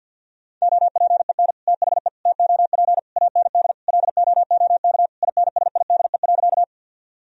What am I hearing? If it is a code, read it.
Morse code, heard as a keyed tone, about 50 wpm, 700 Hz. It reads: OPEN THE TOP AND LOOK INSIDE.
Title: Sentences from Top 1000 Words - Review Part 2 - 50wpm